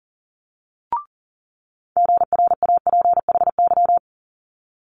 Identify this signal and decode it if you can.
Morse code, heard as a keyed tone, about 38 wpm, 700 Hz. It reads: GRAPHY
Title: Multi-letter Phonemes Quick Fire 40wpm